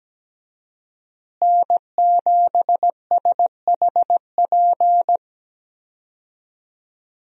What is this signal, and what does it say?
Morse code, heard as a keyed tone, about 17 wpm, 700 Hz. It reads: N7SHP
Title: USA Call Signs - 5 Characters - Repeated 2x - 17wpm